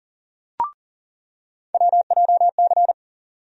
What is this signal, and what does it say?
Morse code, heard as a keyed tone, about 40 wpm, 700 Hz. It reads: WJC
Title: USA Call Sign Postfixes - 3 Letters 40wpm